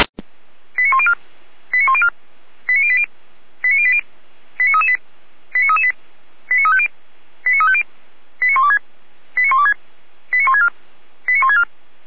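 Five-tone selective call (ZVEI), heard as quick runs of five tones, tones of about 70 ms. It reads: 89104 89104 89080 89080 89338 89338 89350 89350 89126 89126 89164 89164
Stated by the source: ZVEI five-tone call received by SDR.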